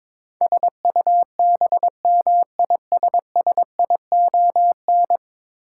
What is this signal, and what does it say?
Morse code, heard as a keyed tone, about 21 wpm, 700 Hz. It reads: SUBMISSION
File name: practice / 22wpm / 10 Letter Words - First 3 Letters Given - 22wpm